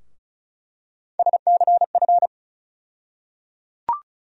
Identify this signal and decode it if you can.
Morse code, heard as a keyed tone, about 36 wpm, 700 Hz. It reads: SCF
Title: USA Call Sign Postfixes - 3 Letters 35wpm